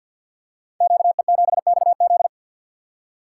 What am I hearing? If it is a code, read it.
Morse code, heard as a keyed tone, about 52 wpm, 700 Hz. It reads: QE7XZ
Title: Code Groups - 5 Letter:Numbers - 50wpm